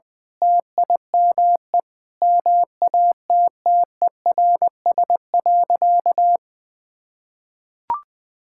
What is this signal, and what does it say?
Morse code, heard as a keyed tone, about 20 wpm, 700 Hz. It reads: TIME MATTERS.
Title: Sentences with 9-Word Phrases - 20wpm